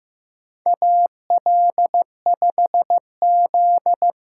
Morse code, heard as a keyed tone, about 15 wpm, 700 Hz. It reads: AL5Z
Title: USA Call Signs - 4x Character Spacing - 15wpm